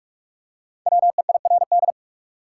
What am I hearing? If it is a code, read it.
Morse code, heard as a keyed tone, about 46 wpm, 700 Hz. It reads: WEIRD
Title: Five Letter Words - 45wpm